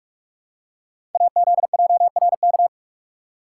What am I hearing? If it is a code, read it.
Morse code, heard as a keyed tone, about 44 wpm, 700 Hz. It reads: AZJRK
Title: Code Groups - 5 Letter:Numbers - 45wpm